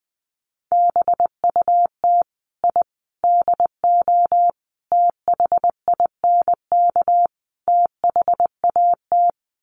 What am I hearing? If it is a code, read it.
Morse code, heard as a keyed tone, about 20 wpm, 700 Hz. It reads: BUT I DO THINK THAT
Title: Sets of 5 Words Encore2 20wpm